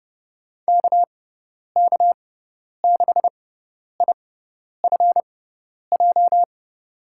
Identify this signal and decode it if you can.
Morse code, heard as a keyed tone, about 30 wpm, spaced as for 10 wpm, 700 Hz. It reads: KK6IFJ
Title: USA Call Signs - 6 Characters - 6x Character Spacing - 30wpm